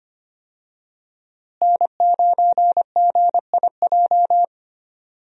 Morse code, heard as a keyed tone, about 25 wpm, 700 Hz. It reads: N9GIJ
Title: USA Call Signs - 5 Characters - Repeated 2x - 25wpm